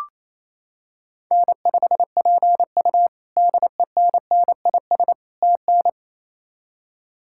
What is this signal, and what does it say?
Morse code, heard as a keyed tone, about 27 wpm, 700 Hz. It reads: N5PU DENNIS TN
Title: K1USN SST QSO Practice - 28wpm